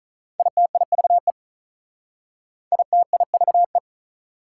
Morse code, heard as a keyed tone, about 45 wpm, 690 Hz. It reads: ITIVE ITIVE